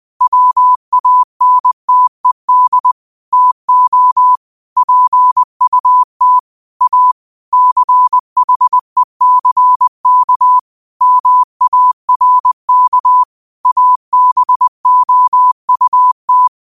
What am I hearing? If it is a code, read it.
Morse code, heard as a keyed tone, about 20 wpm, 1000 Hz. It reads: WANTED TO PUT A CHECK MARK ABOUT
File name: qotdfiles / 20wpm